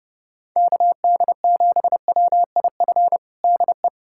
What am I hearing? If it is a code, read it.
Morse code, heard as a keyed tone, about 30 wpm, 700 Hz. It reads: KD7WIF DE
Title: USA Callsign DE Callsign - 6 Characters - 30wpm